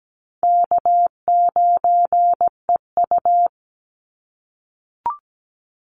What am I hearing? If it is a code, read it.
Morse code, heard as a keyed tone, about 17 wpm, 700 Hz. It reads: K9EU
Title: Easy Call Signs - 17wpm